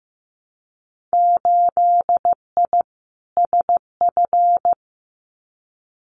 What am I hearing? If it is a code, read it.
Morse code, heard as a keyed tone, about 15 wpm, 700 Hz. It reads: 8I SF
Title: Winter Field Day Exchanges Encore 15wpm